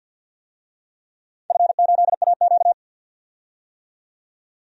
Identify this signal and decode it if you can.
Morse code, heard as a keyed tone, about 53 wpm, 700 Hz. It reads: F8AQ
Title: DXCC-100 Call Signs - 4 Characters - 50wpm